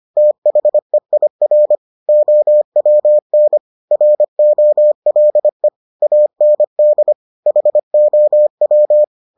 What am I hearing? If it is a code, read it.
Morse code, heard as a keyed tone, about 25 wpm, 600 Hz. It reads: THEIR OWN ROLE AND HOW